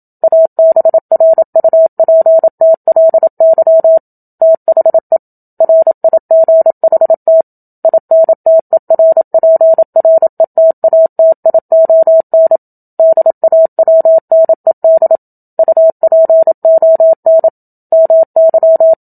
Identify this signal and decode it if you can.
Morse code, heard as a keyed tone, about 27 wpm, 650 Hz. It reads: ABRUPTLY THE RIGHT INTERPRETATION DAWNED UPON MY
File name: Book Two / 01-Chapter 1 - 27 WPM